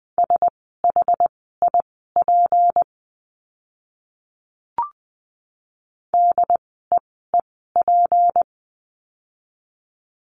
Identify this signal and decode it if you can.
Morse code, heard as a keyed tone, about 20 wpm, spaced as for 14 wpm, 700 Hz. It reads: SHIP DEEP
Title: 153 Lesson 25f - Letter p - Words 20wpm